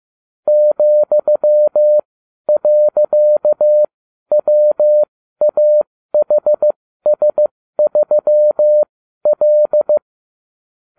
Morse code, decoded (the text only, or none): ,.WAHS3L